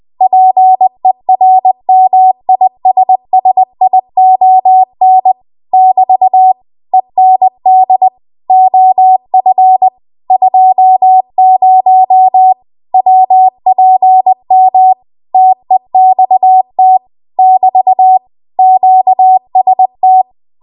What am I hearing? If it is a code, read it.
Morse code, heard as a keyed tone, about 20 wpm, 750 Hz. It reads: PERMISSION <BT> END OF 20 WPM TEXT <BT> QST
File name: Archive / 20 WPM / 20WPM